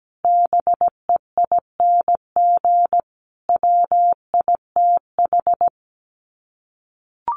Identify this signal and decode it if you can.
Morse code, heard as a keyed tone, about 17 wpm, 700 Hz. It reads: BEING WITH
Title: Sets of 5 Words Encore 17wpm